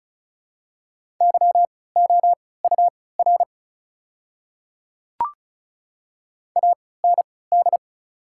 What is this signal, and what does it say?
Morse code, heard as a keyed tone, about 34 wpm, spaced as for 20 wpm, 690 Hz. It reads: YOUR AND
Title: 3x Character Spacing - Top 100 Words - 35wpm